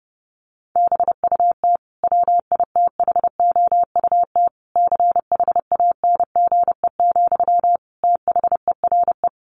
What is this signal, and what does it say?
Morse code, heard as a keyed tone, about 28 wpm, 700 Hz. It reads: BUT WITHOUT CHANGE, THERE
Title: Sentences from Top 1400 Words 30wpm